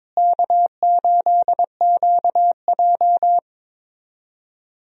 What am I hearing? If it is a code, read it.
Morse code, heard as a keyed tone, about 22 wpm, 700 Hz. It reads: K8QJ
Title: USA Call Signs - 4 Characters - Repeated 3x - 22wpm